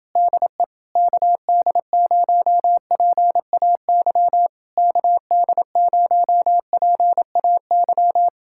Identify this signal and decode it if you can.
Morse code, heard as a keyed tone, about 29 wpm, 700 Hz. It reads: DE KD0PAY KD0PAY